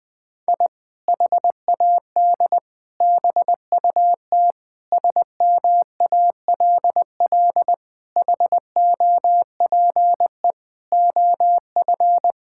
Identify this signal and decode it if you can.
Morse code, heard as a keyed tone, about 20 wpm, 700 Hz. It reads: I HAD BUT SMALL HOPE OF
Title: Sentences from Top 1000 Words - Review Part 6 - 20wpm